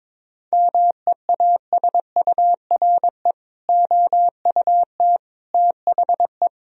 Morse code, heard as a keyed tone, about 22 wpm, 700 Hz. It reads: MEASURE OUT THE
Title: Sentences from Top 1000 Words - Review Part 2 - 22wpm